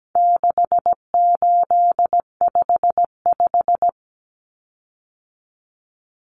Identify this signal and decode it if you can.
Morse code, heard as a keyed tone, about 17 wpm, 700 Hz. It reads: 6855
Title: Common Cut Numbers - 4-Digit 17wpm